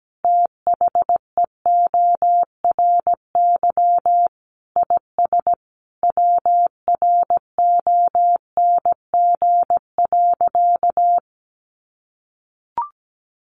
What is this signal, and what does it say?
Morse code, heard as a keyed tone, about 17 wpm, 700 Hz. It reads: THEORY IS WRONG.
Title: A Sound Of Thunder 17wpm